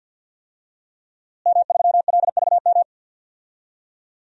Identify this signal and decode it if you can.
Morse code, heard as a keyed tone, about 51 wpm, 700 Hz. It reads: M3LVM
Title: DXCC-100 Call Signs - 5 Characters - Repeated 2x - 50wpm